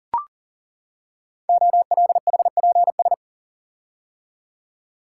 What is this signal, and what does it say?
Morse code, heard as a keyed tone, about 42 wpm, 700 Hz. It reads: OLHPS
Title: Code Groups - 5 Letters - 40wpm